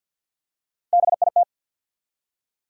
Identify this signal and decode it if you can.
Morse code, heard as a keyed tone, about 48 wpm, 700 Hz. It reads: BIT
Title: Three Letter Words - 50wpm